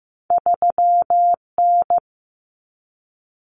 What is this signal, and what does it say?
Morse code, heard as a keyed tone, about 15 wpm, 700 Hz. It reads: 3N
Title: Common Cut Numbers - 2-Digit 15wpm